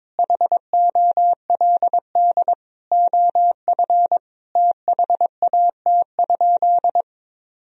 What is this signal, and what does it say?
Morse code, heard as a keyed tone, about 22 wpm, 700 Hz. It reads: HOLD OF THAT?
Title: Sentences from Top 1000 Words - Review Part 2 - 22wpm